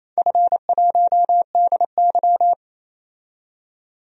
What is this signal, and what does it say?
Morse code, heard as a keyed tone, about 29 wpm, 700 Hz. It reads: F1DY
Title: DXCC-100 Call Signs - 4 Characters - 28wpm